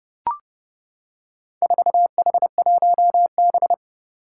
Morse code, heard as a keyed tone, about 32 wpm, 700 Hz. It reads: 4H1B
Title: Four Challenging Letter-Number Combinations 30wpm